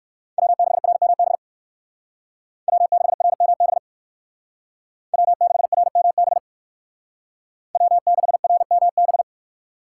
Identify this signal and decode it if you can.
Morse code, heard as a keyed tone, about 56 wpm, 700 Hz. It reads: W6RMB W6RMB W6RMB W6RMB
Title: Speed Racing - USA 1x3 Call Signs 45wpm